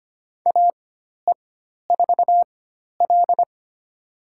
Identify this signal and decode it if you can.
Morse code, heard as a keyed tone, about 25 wpm, spaced as for 12 wpm, 700 Hz. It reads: AE4L
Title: USA Call Signs - 4 Characters - 4x Character Spacing - 25wpm